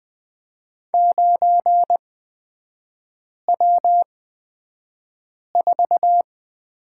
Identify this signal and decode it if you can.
Morse code, heard as a keyed tone, about 20 wpm, spaced as for 5 wpm, 690 Hz. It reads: 9W4
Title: 229 Lesson 37e - Letter x - Three Characters 20wpm